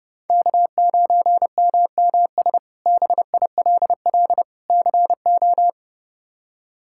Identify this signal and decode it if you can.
Morse code, heard as a keyed tone, about 31 wpm, 700 Hz. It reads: K9MMS BILL CO
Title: K1USN SST QSO Practice - 30wpm